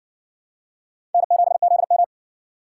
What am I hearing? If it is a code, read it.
Morse code, heard as a keyed tone, about 60 wpm, 700 Hz. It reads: N7ZM